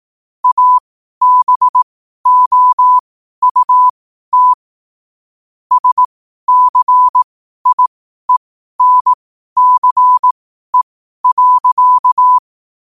Morse code, heard as a keyed tone, about 18 wpm, spaced as for 12 wpm, 1000 Hz. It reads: ABOUT SCIENCE.